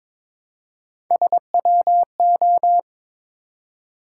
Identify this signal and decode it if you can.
Morse code, heard as a keyed tone, about 22 wpm, 700 Hz. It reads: SWO
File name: practice / 22wpm / Three Letter-Number Combinations - 22wpm